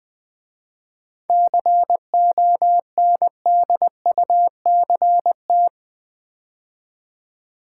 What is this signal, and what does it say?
Morse code, heard as a keyed tone, about 20 wpm, 700 Hz. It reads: CONDUCT
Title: 7 Letter Words - First 3 Letters Given - 20wpm